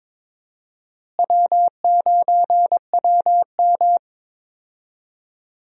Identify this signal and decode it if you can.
Morse code, heard as a keyed tone, about 22 wpm, 700 Hz. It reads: W9WM
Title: USA Call Signs - 4 Characters - Repeated 3x - 22wpm